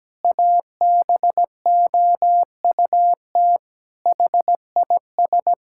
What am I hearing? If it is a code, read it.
Morse code, heard as a keyed tone, about 17 wpm, 700 Hz. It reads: ABOUT HIS
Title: Pangram Challenge v2 - 17wpm